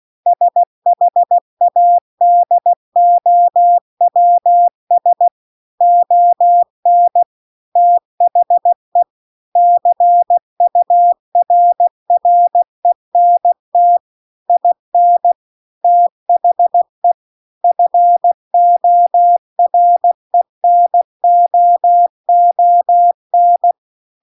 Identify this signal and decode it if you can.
Morse code, heard as a keyed tone, about 16 wpm, 700 Hz. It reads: SHADOWS ON THE CURRENT IN THE FORENOON